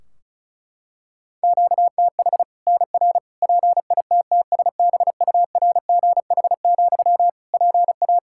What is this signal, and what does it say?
Morse code, heard as a keyed tone, about 36 wpm, 700 Hz. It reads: QTH NR PITTSBURGH, PA
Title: Practice QSO - Exchange 1 - Beginning - 35wpm